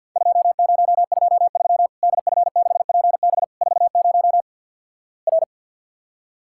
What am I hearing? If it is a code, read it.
Morse code, heard as a keyed tone, about 50 wpm, 700 Hz. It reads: 2023 DF6PB 40 R
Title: Worked All Europe DX Contest - QTC Exchange with Repeats - 50wpm